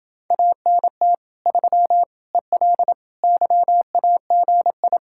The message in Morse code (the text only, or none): ANT 3 EL YAGI